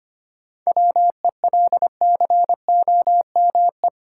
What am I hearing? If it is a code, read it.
Morse code, heard as a keyed tone, about 25 wpm, 700 Hz. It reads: WELCOME
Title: Seven Letter Words - 25wpm